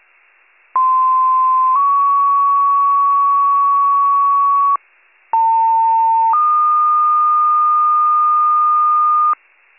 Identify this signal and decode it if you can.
Two-tone page, heard as a one-second tone, then a three-second tone, about 1040 then 1120 Hz, about 900 then 1220 Hz. It is heard from 0.7 to 9.3 s.